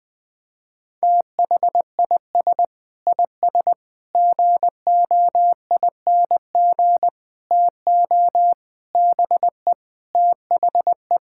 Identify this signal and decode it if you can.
Morse code, heard as a keyed tone, about 20 wpm, 700 Hz. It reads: THIS IS GOING TO BE THE
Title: Sentences with 5-Word Phrases - 20wpm